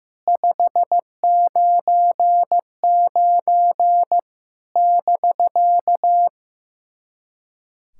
Morse code, heard as a keyed tone, about 15 wpm, 700 Hz. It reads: 599 <BK>
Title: SOTA QSO Practice 15wpm